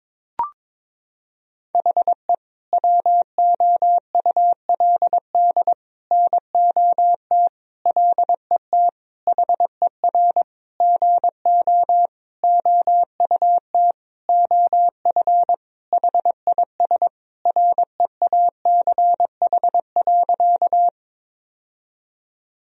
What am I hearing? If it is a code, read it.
Morse code, heard as a keyed tone, about 22 wpm, 700 Hz. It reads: HE WOULD NOT LET HER GO OUT OF HIS REACH.
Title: Sentences from Top 1000 Words - Review Part 2 - 22wpm